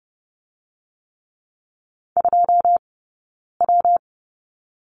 Morse code, heard as a keyed tone, about 30 wpm, 700 Hz. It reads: 2 W
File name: Lesson 19d - Letter w - Two Characters 30wpm